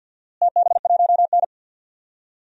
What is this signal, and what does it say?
Morse code, heard as a keyed tone, about 52 wpm, 700 Hz. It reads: TB1N
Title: DXCC-100 Call Signs - 4 Characters - Repeated 3x - 50wpm